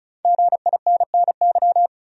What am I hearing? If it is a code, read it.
Morse code, heard as a keyed tone, about 34 wpm, 700 Hz. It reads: GINNY
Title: USA Common Names - Max 5 Letters - 35wpm